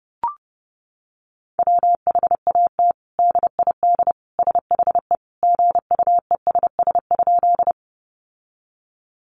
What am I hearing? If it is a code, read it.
Morse code, heard as a keyed tone, about 30 wpm, 700 Hz. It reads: WHAT DID SHE GUESS?
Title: Sentences from Top 1500 Words 30wpm